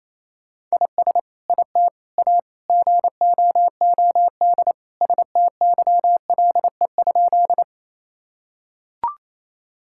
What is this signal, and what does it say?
Morse code, heard as a keyed tone, about 28 wpm, 700 Hz. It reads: IS IT A GOOD STYLE?
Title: Sentences from Top 1000 Words - Review Part 5 - 28wpm